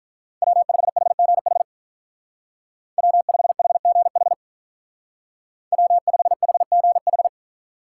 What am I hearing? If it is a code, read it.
Morse code, heard as a keyed tone, about 44 wpm, 700 Hz. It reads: W5HGH W5HGH W5HGH
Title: Speed Racing - USA 1x3 Call Signs 35wpm